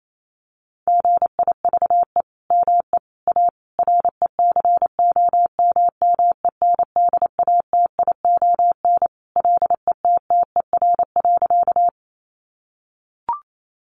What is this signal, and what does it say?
Morse code, heard as a keyed tone, about 28 wpm, 700 Hz. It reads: GIVE ME A RECOMMENDATION LETTER.
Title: Sentences from Top 1400 Words 28wpm